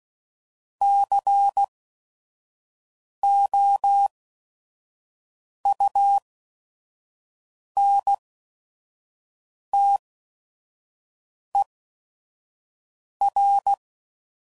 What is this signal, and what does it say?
Morse code, heard as a keyed tone, about 16 wpm, spaced as for 5 wpm, 800 Hz. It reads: COUNTER
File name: CW Code Groups @5WPM / Technical Subjects / XC